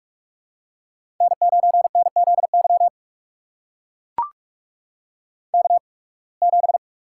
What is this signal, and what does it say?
Morse code, heard as a keyed tone, about 44 wpm, spaced as for 29 wpm, 700 Hz. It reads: N9NZY K7
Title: USA Call Signs - 8x Character Spacing - 45wpm